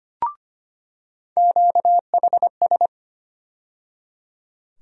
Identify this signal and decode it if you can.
Morse code, heard as a keyed tone, about 25 wpm, 700 Hz. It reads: QHS
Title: USA Call Sign Postfixes - 3 Letters 25wpm